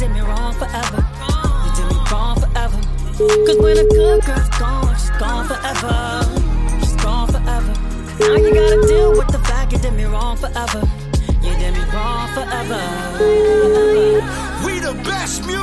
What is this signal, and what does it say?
Ringback tone (Europe, 425 Hz), with noise.